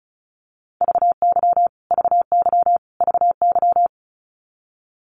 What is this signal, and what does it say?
Morse code, heard as a keyed tone, about 34 wpm, 700 Hz. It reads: VY VY VY
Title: Abbreviations Repeated 3x - 35wpm